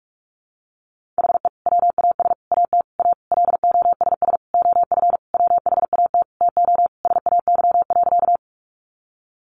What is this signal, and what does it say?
Morse code, heard as a keyed tone, about 44 wpm, 700 Hz. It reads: HE WAS AT A LOSS OF WHAT TO SAY.
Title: Sentences from Top 1400 Words 45wpm